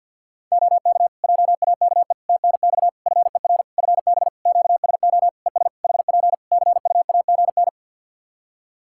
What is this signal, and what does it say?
Morse code, heard as a keyed tone, about 50 wpm, 700 Hz. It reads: OK JAKE TNX FER FB QSO ES HP CUAGN
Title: Practice QSO - Exchange 4 - Ending - 50wpm